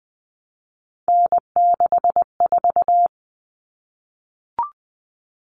Morse code, heard as a keyed tone, about 20 wpm, 700 Hz. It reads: N64
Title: Common Cut Numbers - 3-Digit 20wpm